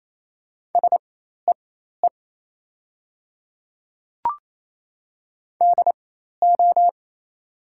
Morse code, heard as a keyed tone, about 28 wpm, spaced as for 13 wpm, 700 Hz. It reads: SEE DO
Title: 4x Character Spacing - Top 300 Words - 28wpm